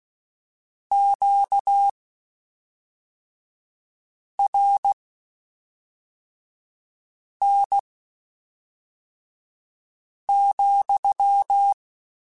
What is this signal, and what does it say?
Morse code, heard as a keyed tone, about 16 wpm, spaced as for 3 wpm, 800 Hz. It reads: QRN,
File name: Q Calls / Q Calls-mp3s / Q Calls sppc @5